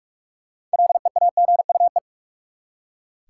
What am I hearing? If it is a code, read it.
Morse code, heard as a keyed tone, about 41 wpm, 700 Hz. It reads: LEAGUE
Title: Six Letter Words - 45wpm